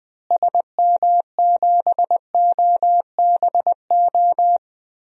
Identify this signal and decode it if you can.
Morse code, heard as a keyed tone, about 20 wpm, 700 Hz. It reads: SM7OBO
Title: DXCC-100 Call Signs - 6 Characters - Repeated 3x - 20wpm